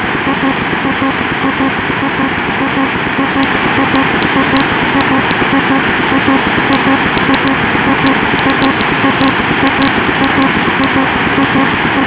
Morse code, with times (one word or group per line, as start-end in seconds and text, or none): none